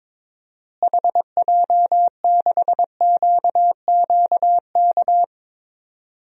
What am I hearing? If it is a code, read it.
Morse code, heard as a keyed tone, about 22 wpm, 700 Hz. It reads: HJ6QQK
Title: DXCC-100 Call Signs - 6 Characters - Repeated 3x - 22wpm